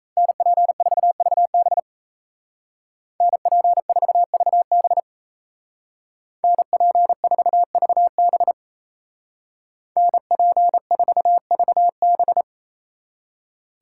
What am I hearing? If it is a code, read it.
Morse code, heard as a keyed tone, about 36 wpm, 700 Hz. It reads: NP4VB NP4VB NP4VB NP4VB